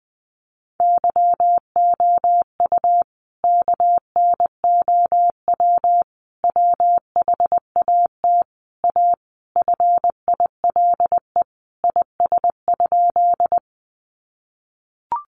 Morse code, heard as a keyed tone, about 20 wpm, 700 Hz. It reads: YOU KNOW WHAT A FILE IS?
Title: Sentences from Top 1200 Words 20wpm